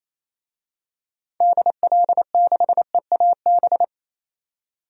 Morse code, heard as a keyed tone, about 28 wpm, 700 Hz. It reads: DL6EAB
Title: DXCC-100 Call Signs - 6 Characters - 28wpm